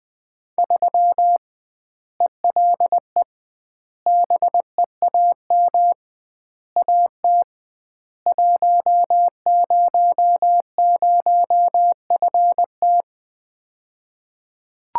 Morse code, heard as a keyed tone, about 20 wpm, 700 Hz. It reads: 3 ELE BEAM AT 100FT
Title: Common QSO Phrases - 2x Word-Spacing - 20wpm